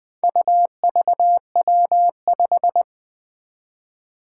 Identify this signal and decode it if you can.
Morse code, heard as a keyed tone, about 20 wpm, 700 Hz. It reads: UVW5
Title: Four Challenging Letter-Number Combinations 20wpm